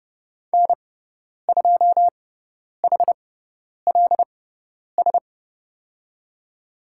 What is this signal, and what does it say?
Morse code, heard as a keyed tone, about 30 wpm, spaced as for 10 wpm, 700 Hz. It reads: N2HLS
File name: Lesson 16g - Letter c - Call Signs 30wpm